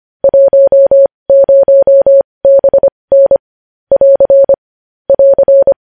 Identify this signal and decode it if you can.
Morse code, heard as a keyed tone, about 25 wpm, spaced as for 20 wpm, 550 Hz. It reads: 10BN <AR> <AR>
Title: News Headlines at 20